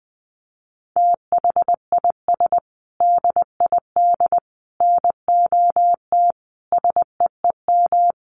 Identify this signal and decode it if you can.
Morse code, heard as a keyed tone, about 19 wpm, 700 Hz. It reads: THIS DID NOT SEEM
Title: Sentences from Top 1400 Words 20wpm